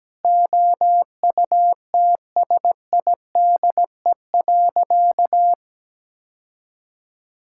Morse code, heard as a keyed tone, about 17 wpm, 700 Hz. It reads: OUTSIDE.
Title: Sentences from Top 1000 Words - Review Part 6 - 17wpm